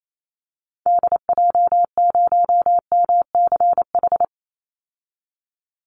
Morse code, heard as a keyed tone, about 28 wpm, 700 Hz. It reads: DJ0MCH
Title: Call Signs Repeated 3x - 28wpm